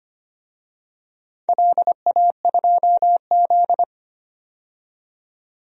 Morse code, heard as a keyed tone, about 25 wpm, 700 Hz. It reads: LA2Z
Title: DXCC-100 Call Signs - 4 Characters - 25wpm